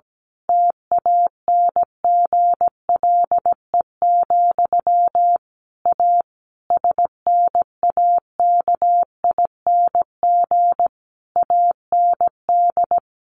Morse code, heard as a keyed tone, about 17 wpm, 700 Hz. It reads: TANGLE, A SNAKING AND
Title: A Sound Of Thunder 17wpm